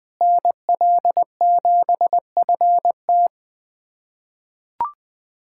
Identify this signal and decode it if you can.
Morse code, heard as a keyed tone, about 20 wpm, 700 Hz. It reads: NL7FT
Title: USA Callsign DE Callsign - 5 Characters - 20wpm